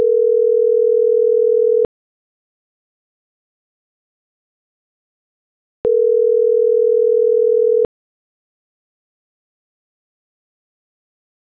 Ringback tone (North America), 440 + 480 Hz.